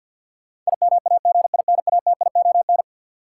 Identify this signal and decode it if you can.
Morse code, heard as a keyed tone, about 53 wpm, 700 Hz. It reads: IMAGINATION
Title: Sentences from Top 1300 Words - New Word Drill Alt - 50wpm